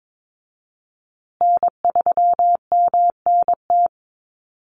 Morse code, heard as a keyed tone, about 22 wpm, 700 Hz. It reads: N3MNT
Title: Call Signs Repeated 2x - 22wpm